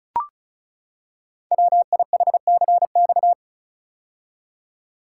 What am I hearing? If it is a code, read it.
Morse code, heard as a keyed tone, about 36 wpm, 700 Hz. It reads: WIHCX
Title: Code Groups - 5 Letters - 35wpm